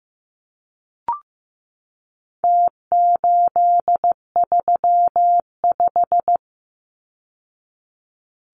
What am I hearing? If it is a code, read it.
Morse code, heard as a keyed tone, about 15 wpm, 700 Hz. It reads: T835
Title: Numbers - Contest Style - 4-Digit 15wpm